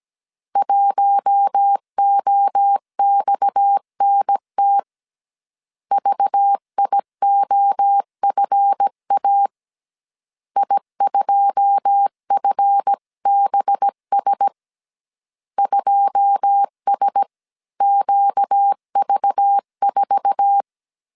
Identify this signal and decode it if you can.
Morse code, heard as a keyed tone, about 17 wpm, 800 Hz. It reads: 1OXNT VIOFA I2FBS 2SQV4
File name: CW / DET / 1 9 1500C S C 16wpm 2909